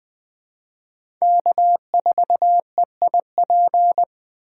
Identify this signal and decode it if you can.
Morse code, heard as a keyed tone, about 20 wpm, 700 Hz. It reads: K4EIP
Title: USA Call Signs - 5 Characters - 4x Character Spacing - 20wpm